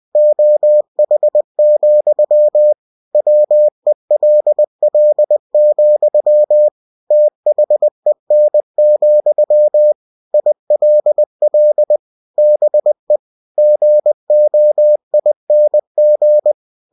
Morse code, heard as a keyed tone, about 20 wpm, 600 Hz. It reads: OH, WELL, THEN, ILL BE GOING
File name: cw-books / winnie-the-pooh / winnie-20wpm-page-0016